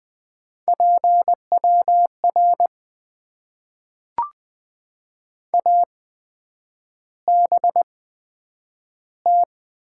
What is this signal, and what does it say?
Morse code, heard as a keyed tone, about 20 wpm, spaced as for 9 wpm, 700 Hz. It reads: PWR ABT
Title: Top 90 QSO Elements - 8x Character Spacing - 20wpm